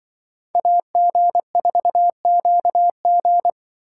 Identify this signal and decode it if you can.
Morse code, heard as a keyed tone, about 24 wpm, 700 Hz. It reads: AG4QG
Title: Speed Racing - USA 2x2 Call Signs 20wpm